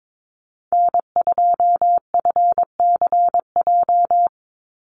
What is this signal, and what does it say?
Morse code, heard as a keyed tone, about 22 wpm, 700 Hz. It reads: N2FCJ
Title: USA Call Signs 22wpm